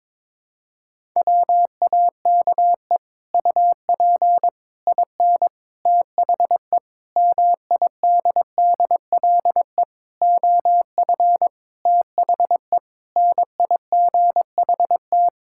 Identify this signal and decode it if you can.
Morse code, heard as a keyed tone, about 22 wpm, 700 Hz. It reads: WAKE UP IN THE MIDDLE OF THE NIGHT
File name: practice / 22wpm / Sets of 8 Words - 22wpm